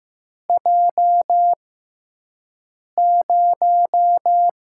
Morse code, heard as a keyed tone, about 15 wpm, spaced as for 5 wpm, 700 Hz. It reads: J0
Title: USA Call Signs - 4 Characters - 6x Character Spacing - 15wpm